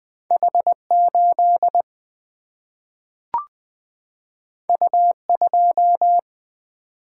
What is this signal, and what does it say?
Morse code, heard as a keyed tone, about 20 wpm, 700 Hz. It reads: H8 U2
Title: Single-Letter Single-Number 20wpm